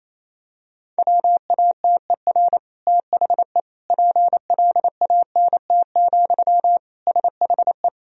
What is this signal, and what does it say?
Morse code, heard as a keyed tone, about 27 wpm, 700 Hz. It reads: WATER THE PLANT, SHE
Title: Sentences from Top 1000 Words - Review Part 1 - 28wpm